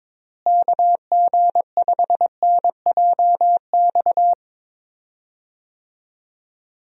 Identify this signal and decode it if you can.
Morse code, heard as a keyed tone, about 22 wpm, 700 Hz. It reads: KG5NJX
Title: USA Call Signs - 6 Characters - Repeated 2x - 22wpm